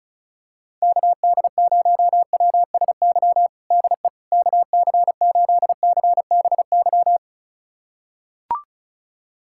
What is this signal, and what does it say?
Morse code, heard as a keyed tone, about 34 wpm, 690 Hz. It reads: KD0WSY DE KC8CBY
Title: USA Callsign DE Callsign 35wpm